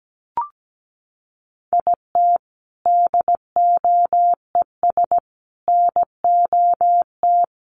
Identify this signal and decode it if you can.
Morse code, heard as a keyed tone, about 17 wpm, 700 Hz. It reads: IT DOES NOT
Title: Sentences from Top 1400 Words 17wpm